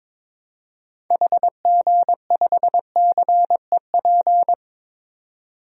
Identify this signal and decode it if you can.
Morse code, heard as a keyed tone, about 22 wpm, 700 Hz. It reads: HG5CEP
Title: DXCC-100 Call Signs - 6 Characters - 22wpm